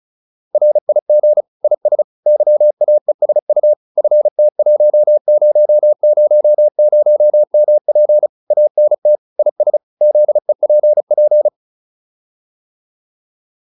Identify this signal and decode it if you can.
Morse code, heard as a keyed tone, about 34 wpm, 600 Hz. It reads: RIG IS YAESU FT1000MP ANT IS ZEPP